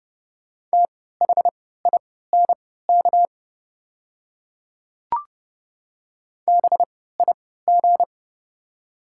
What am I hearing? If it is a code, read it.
Morse code, heard as a keyed tone, about 29 wpm, spaced as for 17 wpm, 690 Hz. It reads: THINK BIG
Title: 3x Character Spacing - Top 300 Words - 30wpm